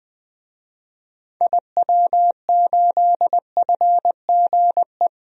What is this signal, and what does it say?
Morse code, heard as a keyed tone, about 20 wpm, 700 Hz. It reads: IW8FGE
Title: DXCC-100 Call Signs - 6 Characters - Repeated 2x - 20wpm